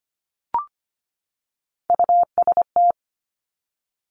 Morse code, heard as a keyed tone, about 25 wpm, 700 Hz. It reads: UST
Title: Top Three Letter Combinations in Words 25wpm